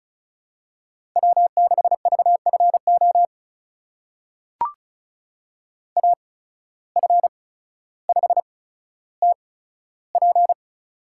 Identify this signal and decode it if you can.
Morse code, heard as a keyed tone, about 34 wpm, spaced as for 15 wpm, 700 Hz. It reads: W6VFO AF5TP
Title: USA Call Signs - 5 Characters - 8x Character Spacing - 35wpm